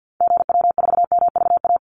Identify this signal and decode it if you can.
Morse code, heard as a keyed tone, about 50 wpm, 700 Hz. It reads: GW4MVA
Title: Call Signs - 50wpm